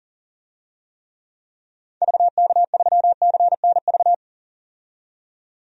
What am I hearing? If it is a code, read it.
Morse code, heard as a keyed tone, about 39 wpm, 700 Hz. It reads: VK3CNV
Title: DXCC-100 Call Signs - 6 Characters - 40wpm